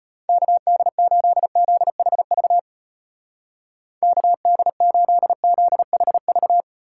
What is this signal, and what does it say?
Morse code, heard as a keyed tone, about 33 wpm, 700 Hz. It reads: KD8ZHV KD8ZHV